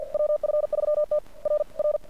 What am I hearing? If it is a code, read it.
Morse code, heard as a keyed tone, about 48 wpm, 600 Hz. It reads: WF3T R R
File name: archived / CRSnet